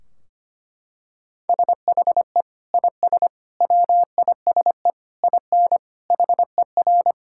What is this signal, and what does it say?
Morse code, heard as a keyed tone, about 25 wpm, 700 Hz. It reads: SHE IS WISE IN HER
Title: Sentences from Top 1600 Words 25wpm